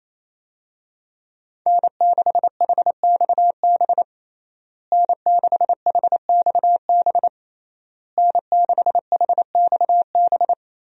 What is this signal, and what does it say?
Morse code, heard as a keyed tone, about 28 wpm, 700 Hz. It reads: N6HXB N6HXB N6HXB
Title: USA Call Signs - 5 Characters - Repeated 3x - 28wpm